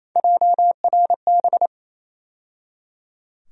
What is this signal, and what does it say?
Morse code, heard as a keyed tone, about 28 wpm, 700 Hz. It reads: JRB
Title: USA Call Sign Postfixes - 3 Letters 28wpm